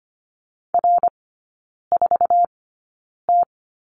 Morse code, heard as a keyed tone, about 25 wpm, spaced as for 9 wpm, 700 Hz. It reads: R4T
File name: course / 25:10wpm / 093 Lesson 16e - Letter c - Three Characters 25wpm